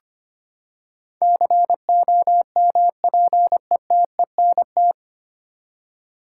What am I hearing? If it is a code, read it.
Morse code, heard as a keyed tone, about 25 wpm, 700 Hz. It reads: COMPETENT
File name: Nine Letter Words - 25wpm